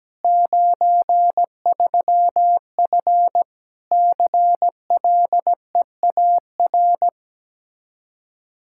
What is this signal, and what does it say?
Morse code, heard as a keyed tone, about 17 wpm, 700 Hz. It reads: 93F CLEAR
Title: Practice QSO - Exchange 2 - Info - 17wpm